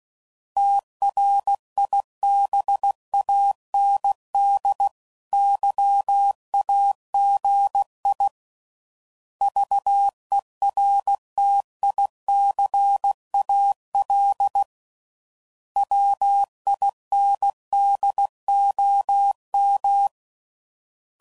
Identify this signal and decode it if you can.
Morse code, heard as a keyed tone, about 16 wpm, 800 Hz. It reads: TRIBANDYAGI VERTICAL WINDOM
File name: CW Code Groups @15WPM / fromSM's / Antennas